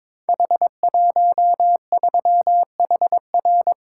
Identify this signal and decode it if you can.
Morse code, heard as a keyed tone, about 23 wpm, 700 Hz. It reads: H13HR
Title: Code Groups - 5 Letter:Numbers - 22wpm